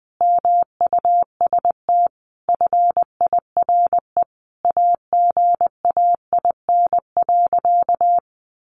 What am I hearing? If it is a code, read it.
Morse code, heard as a keyed tone, about 20 wpm, 700 Hz. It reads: MUST FIRE AGAIN.
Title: Sentences from Top 1400 Words 20wpm